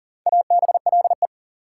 Morse code, heard as a keyed tone, about 40 wpm, 700 Hz. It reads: ABLE
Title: Four Letter Words - 40wpm